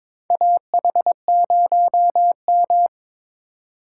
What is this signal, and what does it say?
Morse code, heard as a keyed tone, about 22 wpm, 700 Hz. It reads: AH0M
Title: USA Call Signs - 4 Characters - 3x Character Spacing - 22wpm